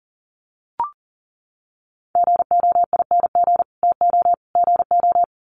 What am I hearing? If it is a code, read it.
Morse code, heard as a keyed tone, about 40 wpm, 700 Hz. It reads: GOING TO GO
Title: Sets of 3 Words 40wpm